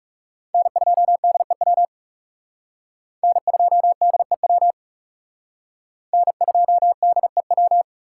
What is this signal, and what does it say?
Morse code, heard as a keyed tone, about 37 wpm, 700 Hz. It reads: N2DEW N2DEW N2DEW
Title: Speed Racing - USA Call Signs 30wpm